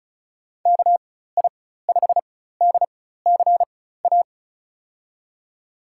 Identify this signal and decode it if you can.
Morse code, heard as a keyed tone, about 36 wpm, spaced as for 16 wpm, 700 Hz. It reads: KI5DCA
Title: USA Call Signs - 6 Characters - 4x Character Spacing - 35wpm